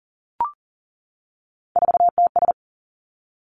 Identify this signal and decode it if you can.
Morse code, heard as a keyed tone, about 39 wpm, 700 Hz. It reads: 4TS
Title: Three Challenging Letter-Number Combinations 40wpm